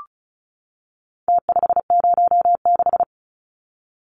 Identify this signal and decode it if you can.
Morse code, heard as a keyed tone, about 34 wpm, 700 Hz. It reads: T506
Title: Numbers - Contest Style - 4-Digit 35wpm